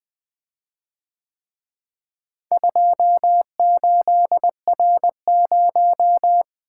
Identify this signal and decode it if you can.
Morse code, heard as a keyed tone, about 23 wpm, 700 Hz. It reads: 28R0
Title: Speed-Racing - Four Letter-Number Combinations - 20wpm